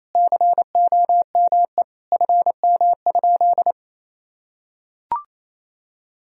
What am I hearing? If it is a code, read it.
Morse code, heard as a keyed tone, about 28 wpm, 700 Hz. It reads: COME FM?
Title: Sentences from Top 200 Words and Abbreviations 28wpm